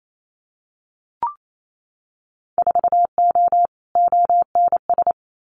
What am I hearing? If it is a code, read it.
Morse code, heard as a keyed tone, about 28 wpm, 700 Hz. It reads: 4O ONS
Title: Winter Field Day Exchanges 28wpm